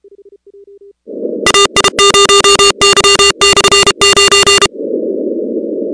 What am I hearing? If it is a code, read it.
Morse code, heard as a keyed tone, about 32 wpm, 2800 Hz, with noise. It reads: AI0Y/9
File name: contests / 2019-cw-wpx-kt5j-k5pi- -2330 0-17